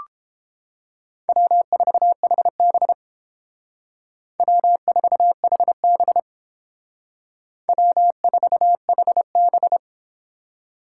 Morse code, heard as a keyed tone, about 31 wpm, 700 Hz. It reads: W4HB W4HB W4HB